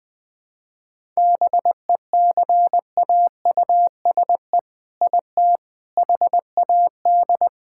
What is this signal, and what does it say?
Morse code, heard as a keyed tone, about 20 wpm, 700 Hz. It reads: BECAUSE IT HAD
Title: All-Age Appropriate Jokes - 20wpm